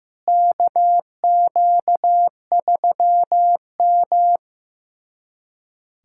Morse code, heard as a keyed tone, about 15 wpm, 700 Hz. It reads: KQ3M
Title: USA Callsign DE Callsign - 4 Characters - 15wpm